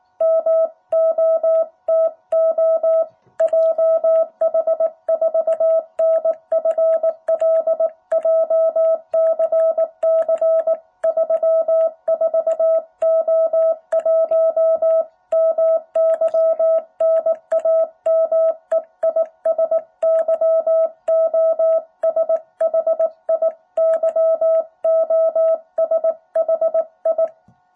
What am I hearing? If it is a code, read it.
Morse code, heard as a keyed tone, about 18 wpm, 660 Hz. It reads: MOTO JH4NFL JCC 34O1 MY NAME IS YOSHI YOSHI
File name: jh8cbh / 2021cw